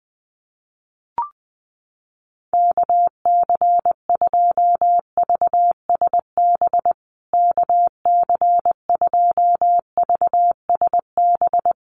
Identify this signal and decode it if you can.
Morse code, heard as a keyed tone, about 19 wpm, 700 Hz. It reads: KC2VSB KC2VSB
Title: USA Call Signs Repeated 3x 20wpm